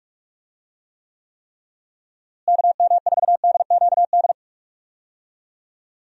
Morse code, heard as a keyed tone, about 46 wpm, 700 Hz. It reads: KM4DQD
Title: USA Call Signs - 6 Characters - Repeated 3x - 45wpm